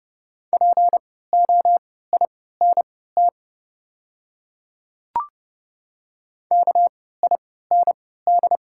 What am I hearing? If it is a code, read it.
Morse code, heard as a keyed tone, about 31 wpm, spaced as for 17 wpm, 700 Hz. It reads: POINT KIND
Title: 3x Character Spacing - Top 300 Words - 30wpm